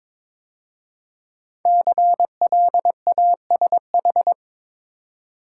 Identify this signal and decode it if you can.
Morse code, heard as a keyed tone, about 23 wpm, 690 Hz. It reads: CLASH
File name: 5 Letter Words - First 3 Letters Given - 22wpm